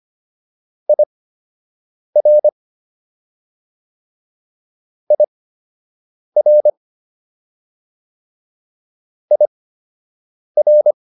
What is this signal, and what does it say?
Morse code, heard as a keyed tone, about 25 wpm, spaced as for 7 wpm, 600 Hz. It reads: IR IR IR